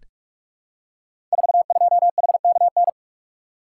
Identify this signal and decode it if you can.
Morse code, heard as a keyed tone, about 42 wpm, 700 Hz. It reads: 42HKN
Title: Code Groups - 5 Letter:Numbers - 45wpm